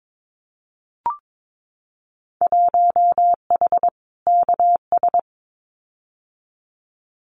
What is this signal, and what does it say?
Morse code, heard as a keyed tone, about 22 wpm, 700 Hz. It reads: 1H KS
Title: Winter Field Day Exchanges 22wpm